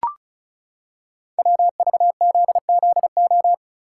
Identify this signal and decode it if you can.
Morse code, heard as a keyed tone, about 36 wpm, 690 Hz. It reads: WVZZO
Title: Code Groups - 5 Letters - 35wpm